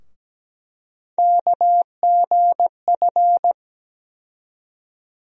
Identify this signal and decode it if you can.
Morse code, heard as a keyed tone, about 17 wpm, 700 Hz. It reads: KGF
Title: USA Call Sign Postfixes - 3 Letters 17wpm